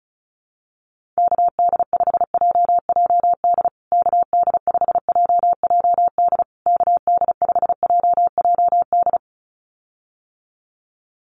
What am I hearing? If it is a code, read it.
Morse code, heard as a keyed tone, about 34 wpm, 700 Hz. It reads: KD5JJD KD5JJD KD5JJD
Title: USA Call Signs Repeated 3x 35wpm